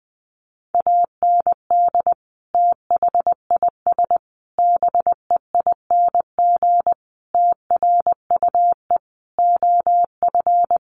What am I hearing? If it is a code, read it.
Morse code, heard as a keyed tone, about 20 wpm, 700 Hz. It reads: AND THIS BEING TRUE OF
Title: Sentences from Top 1400 Words 20wpm